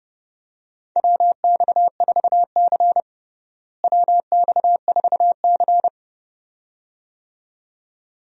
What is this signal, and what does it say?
Morse code, heard as a keyed tone, about 30 wpm, 700 Hz. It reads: WX4C WX4C
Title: USA Call Signs - 4 Characters - Repeated 2x - 30wpm